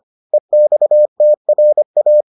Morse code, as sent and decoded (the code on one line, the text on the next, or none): . -..- - .-. .-
EXTRA